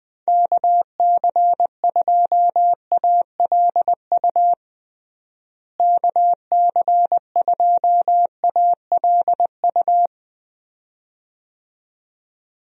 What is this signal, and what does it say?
Morse code, heard as a keyed tone, about 21 wpm, 700 Hz. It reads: KC2ALU KC2ALU
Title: USA Call Signs - 6 Characters - Repeated 2x - 20wpm